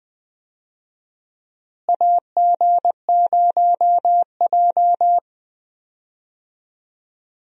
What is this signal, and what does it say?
Morse code, heard as a keyed tone, about 20 wpm, 700 Hz. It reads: AG0J
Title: USA Call Signs - 4 Characters - Repeated 2x - 20wpm